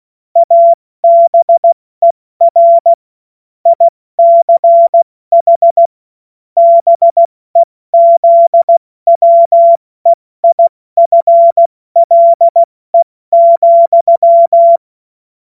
Morse code, heard as a keyed tone, about 16 wpm, 680 Hz. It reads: ABER ICH BEZWEIFLE,
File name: Morsekurs / Witze / Witz 23. Dezember 2025 70 F80